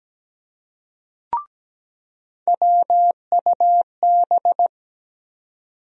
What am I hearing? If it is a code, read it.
Morse code, heard as a keyed tone, about 17 wpm, 700 Hz. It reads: WUB
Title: USA Call Sign Postfixes - 3 Letters 17wpm